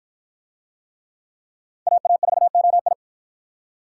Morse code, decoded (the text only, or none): AA4OI